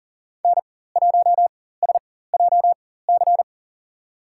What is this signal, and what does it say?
Morse code, heard as a keyed tone, about 40 wpm, spaced as for 19 wpm, 700 Hz. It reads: N1SJC
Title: USA Call Signs - 5 Characters - 4x Character Spacing - 40wpm